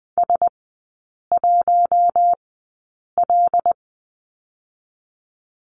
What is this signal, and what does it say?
Morse code, heard as a keyed tone, about 20 wpm, spaced as for 8 wpm, 700 Hz. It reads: S1L
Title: 107 Lesson 18e - Letter m - Three Characters 20wpm